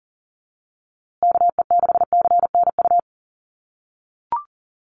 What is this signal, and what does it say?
Morse code, heard as a keyed tone, about 38 wpm, 700 Hz. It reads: KE6CNU
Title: USA Call Signs Repeated 2x 40wpm